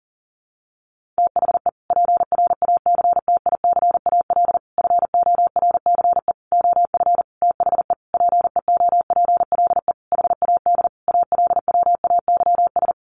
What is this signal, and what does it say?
Morse code, heard as a keyed tone, about 40 wpm, 700 Hz. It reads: THE PRACTICAL FORCE OF THE PEOPLE HAD ALWAYS
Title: Sentences from Top 1400 Words 40wpm